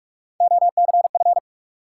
Morse code, heard as a keyed tone, about 42 wpm, 700 Hz. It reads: OCF
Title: Top 180 QSO Elements 45wpm